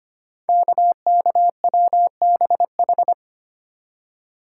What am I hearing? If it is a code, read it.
Morse code, heard as a keyed tone, about 25 wpm, 700 Hz. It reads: KKWBH
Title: Code Groups - 5 Letters - 25wpm